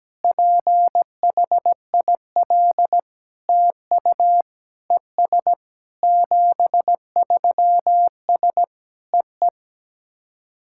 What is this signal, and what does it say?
Morse code, heard as a keyed tone, about 17 wpm, 700 Hz. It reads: PHIL TU ES 73S EE
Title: SOTA QSO Practice 17wpm